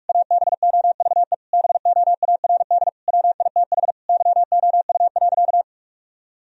Morse code, heard as a keyed tone, about 46 wpm, 700 Hz. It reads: ABOVE BOARD WITH YOU.
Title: Sentences from Top 1000 Words - Review Part 4 - 45wpm